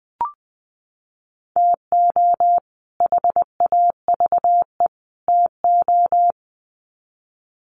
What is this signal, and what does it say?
Morse code, heard as a keyed tone, about 20 wpm, 700 Hz. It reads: TO HAVE TO
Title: Sets of 3 Words 20wpm